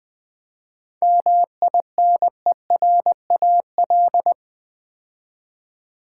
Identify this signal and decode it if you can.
Morse code, heard as a keyed tone, about 20 wpm, 700 Hz. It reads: MINERAL
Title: 7 Letter Words - First 3 Letters Given - 20wpm